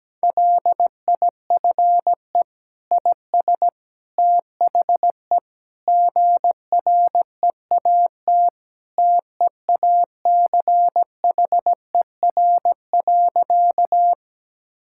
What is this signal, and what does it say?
Morse code, heard as a keyed tone, about 17 wpm, 700 Hz. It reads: LIFE IS THE GREAT TEACHER.
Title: Sentences from Top 1000 Words - Review Part 6 - 17wpm